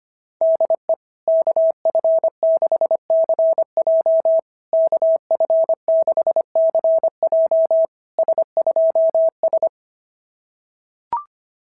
Morse code, heard as a keyed tone, about 24 wpm, 650 Hz. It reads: DE KF6CJ KF6CJ S2S
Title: SOTA QSO Practice 25wpm